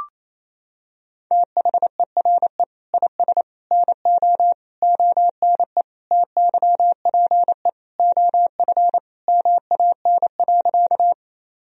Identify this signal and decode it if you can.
Morse code, heard as a keyed tone, about 28 wpm, 700 Hz. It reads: THERE IS NO ONE TYPE OF MAN.
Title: Sentences from Top 1000 Words - Review Part 3 - 28wpm